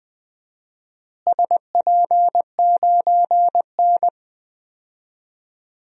Morse code, heard as a keyed tone, about 20 wpm, 700 Hz. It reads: SP9N
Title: DXCC-100 Call Signs - 4 Characters - Repeated 2x - 20wpm